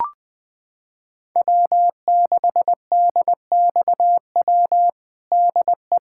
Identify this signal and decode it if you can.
Morse code, heard as a keyed tone, about 21 wpm, 700 Hz. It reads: W6DXW DE
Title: USA Callsign DE Callsign - 5 Characters - 20wpm